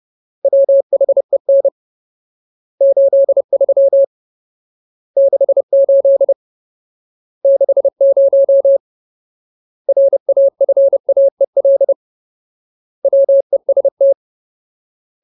Morse code, heard as a keyed tone, about 30 wpm, 550 Hz. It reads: WHEN 83 68 60 RAFAEL WEST
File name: Ultimate Word List / 30WPM Ultimate